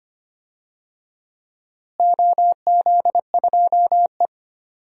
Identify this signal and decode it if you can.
Morse code, heard as a keyed tone, about 25 wpm, 700 Hz. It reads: OZ2E
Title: DXCC-100 Call Signs - 4 Characters - Repeated 2x - 25wpm